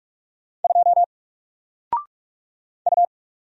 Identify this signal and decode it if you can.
Morse code, heard as a keyed tone, about 46 wpm, 700 Hz. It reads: 2 U
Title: Single Letter-Number 45wpm